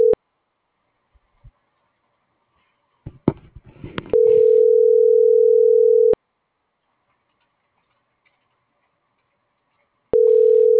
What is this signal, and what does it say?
Ringback tone (North America), 440 + 480 Hz.
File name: CENTRAL GENERAL / out alexander.sanchez